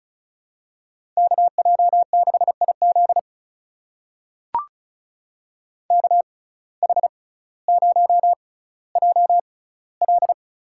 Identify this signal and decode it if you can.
Morse code, heard as a keyed tone, about 36 wpm, spaced as for 18 wpm, 700 Hz. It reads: KJ6IZ KH0JL
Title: USA Call Signs - 5 Characters - 6x Character Spacing - 35wpm